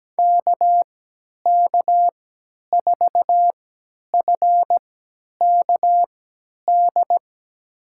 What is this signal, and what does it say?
Morse code, heard as a keyed tone, about 17 wpm, spaced as for 10 wpm, 700 Hz. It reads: KK4FKD
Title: USA Call Signs - 6 Characters - 3x Character Spacing - 17wpm